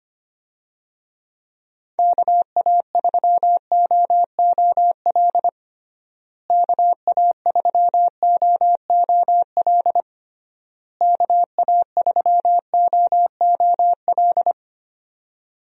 Morse code, heard as a keyed tone, about 25 wpm, 700 Hz. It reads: KA3OOL KA3OOL KA3OOL
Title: USA Call Signs - 6 Characters - Repeated 3x - 25wpm